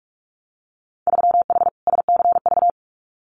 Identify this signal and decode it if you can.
Morse code, heard as a keyed tone, about 46 wpm, 690 Hz. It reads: 3H SCV
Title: Winter Field Day Exchanges Encore 45wpm